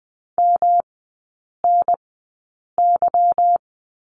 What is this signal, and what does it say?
Morse code, heard as a keyed tone, about 20 wpm, spaced as for 8 wpm, 690 Hz. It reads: MNY
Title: 145 Lesson 24e - Letter y - Three Characters 20wpm